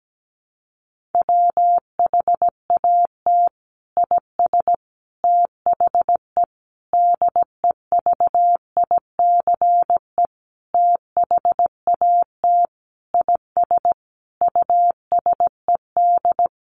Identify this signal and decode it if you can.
Morse code, heard as a keyed tone, about 17 wpm, 700 Hz. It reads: WHAT IS THE DEVICE THAT IS USED
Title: Sentences from Top 1500 Words 17wpm